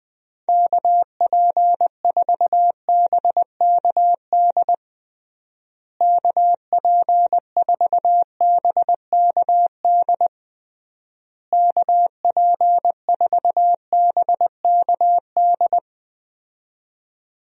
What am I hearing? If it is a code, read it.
Morse code, heard as a keyed tone, about 21 wpm, 700 Hz. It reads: KP4BKD KP4BKD KP4BKD
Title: DXCC-100 Call Signs - 6 Characters - Repeated 3x - 20wpm